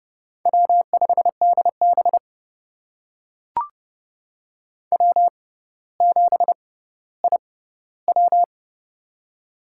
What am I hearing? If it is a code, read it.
Morse code, heard as a keyed tone, about 32 wpm, spaced as for 16 wpm, 700 Hz. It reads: W5DB W7IW
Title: USA Call Signs - 4 Characters - 6x Character Spacing - 30wpm